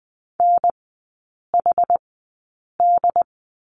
Morse code, heard as a keyed tone, about 20 wpm, spaced as for 8 wpm, 700 Hz. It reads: NHD
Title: 067 Lesson 12e - Letter d - Three Characters 20wpm